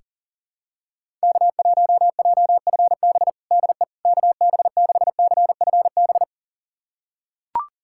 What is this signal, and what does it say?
Morse code, heard as a keyed tone, about 40 wpm, 700 Hz. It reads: K1JFB DE KB6CFB
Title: USA Callsign DE Callsign 40wpm